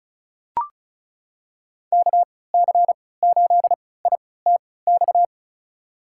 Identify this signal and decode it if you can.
Morse code, heard as a keyed tone, about 35 wpm, spaced as for 20 wpm, 700 Hz. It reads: KC8ITX
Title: USA Call Signs - 6 Characters - 3x Character Spacing - 35wpm